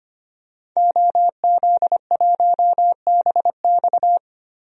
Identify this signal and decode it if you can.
Morse code, heard as a keyed tone, about 25 wpm, 700 Hz. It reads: OZ1BX
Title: DXCC-100 Call Signs - 5 Characters - 25wpm